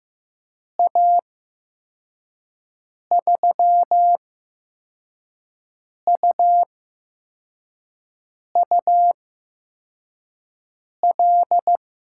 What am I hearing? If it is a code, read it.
Morse code, heard as a keyed tone, about 15 wpm, spaced as for 4 wpm, 700 Hz. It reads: A3UUL
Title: USA Call Signs - 6 Characters - 8x Character Spacing - 15wpm